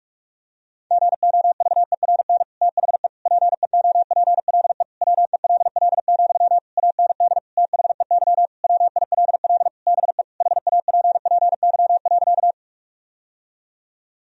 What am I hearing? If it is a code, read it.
Morse code, heard as a keyed tone, about 46 wpm, 700 Hz. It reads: GOVERN THE PEOPLE WELL, AND THEY WILL BE HAPPY.
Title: Sentences from Top 1000 Words - Review Part 2 - 45wpm